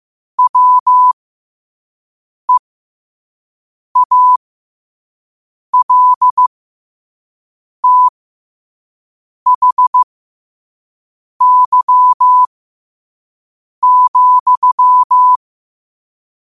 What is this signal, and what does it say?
Morse code, heard as a keyed tone, about 15 wpm, spaced as for 5 wpm, 1000 Hz. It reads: WEALTHY,
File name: qotdfiles / 5wpm